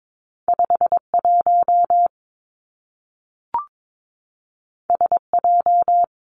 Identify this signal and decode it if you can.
Morse code, heard as a keyed tone, about 22 wpm, 700 Hz. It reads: 5 1 S J
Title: Two Challenging Letter-Number Combinations 22wpm